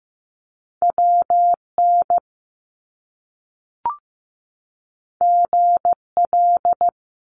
Morse code, heard as a keyed tone, about 15 wpm, 700 Hz. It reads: WN GL
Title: Top Two Letter Combinations in Words 15wpm